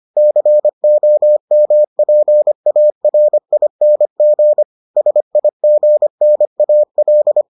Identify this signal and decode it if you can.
Morse code, heard as a keyed tone, about 25 wpm, 600 Hz. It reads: COMPARING SIGNAL